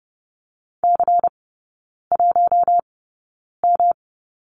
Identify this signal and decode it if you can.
Morse code, heard as a keyed tone, about 30 wpm, spaced as for 9 wpm, 700 Hz. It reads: C1M
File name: Lesson 18e - Letter m - Three Characters 30wpm